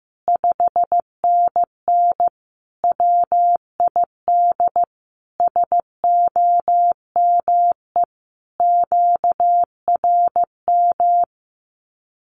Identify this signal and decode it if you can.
Morse code, heard as a keyed tone, about 15 wpm, 700 Hz. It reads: 5NN WID SOME QRM
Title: Common QSO Phrases 15wpm